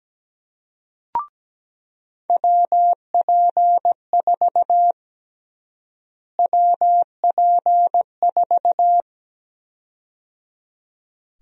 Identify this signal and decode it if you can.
Morse code, heard as a keyed tone, about 17 wpm, 700 Hz. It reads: WP4 WP4
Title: USA Call Sign Prefixes - Repeated 2x - 17wpm